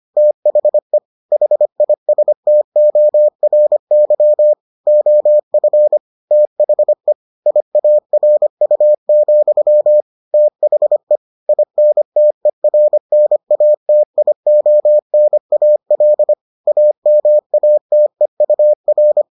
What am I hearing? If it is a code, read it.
Morse code, heard as a keyed tone, about 25 wpm, 600 Hz. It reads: THE HISTORY OF THE IARU, THE INTERNATIONAL AMATEUR